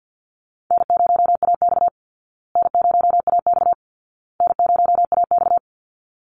Morse code, heard as a keyed tone, about 51 wpm, 700 Hz. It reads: N0AX N0AX N0AX
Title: Call Signs Repeated 2x - 50wpm